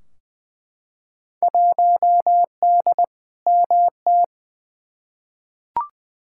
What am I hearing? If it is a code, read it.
Morse code, heard as a keyed tone, about 20 wpm, 700 Hz. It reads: 1D MT
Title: Field Day Exchanges - 20wpm